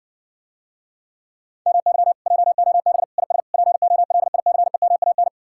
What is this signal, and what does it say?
Morse code, heard as a keyed tone, about 60 wpm, 700 Hz. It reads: MY JOB IS POLICEMAN